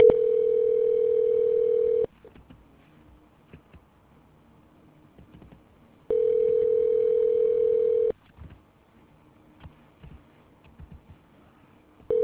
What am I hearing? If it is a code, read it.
Ringback tone (North America), 440 + 480 Hz.